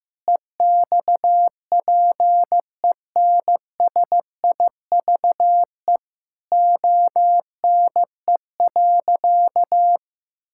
Morse code, heard as a keyed tone, about 15 wpm, 700 Hz. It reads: EXPENSIVE ONE.
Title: Sentences from Top 1000 Words - Review Part 6 - 15wpm